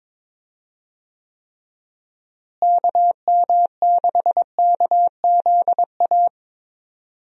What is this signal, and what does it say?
Morse code, heard as a keyed tone, about 22 wpm, 700 Hz. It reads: KM6KZA
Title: USA Call Signs - 6 Characters - Repeated 2x - 22wpm